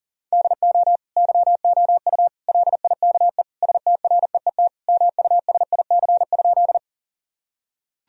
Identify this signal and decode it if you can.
Morse code, heard as a keyed tone, about 40 wpm, 700 Hz. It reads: DO YOU LIKE STREET MUSIC?
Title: Sentences from Top 1000 Words - Review Part 3 - 40wpm